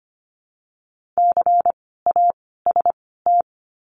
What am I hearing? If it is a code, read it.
Morse code, heard as a keyed tone, about 25 wpm, 700 Hz. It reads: C A S T
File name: course / 25:10wpm / 094 Lesson 16f - Letter c - Words 25wpm